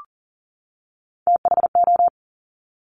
Morse code, heard as a keyed tone, about 42 wpm, 700 Hz. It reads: THO
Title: Top Three Letter Combinations in Words 40wpm